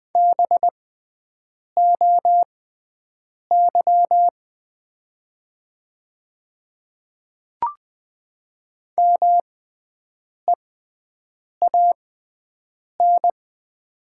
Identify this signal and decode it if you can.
Morse code, heard as a keyed tone, about 20 wpm, spaced as for 7 wpm, 700 Hz. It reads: BOY MEAN
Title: 6x Character Spacing - Top 200 Words - 20wpm